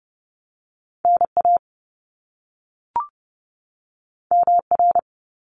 Morse code, heard as a keyed tone, about 30 wpm, 700 Hz. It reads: NA MR
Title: Top Two Letter Combinations in Words 30wpm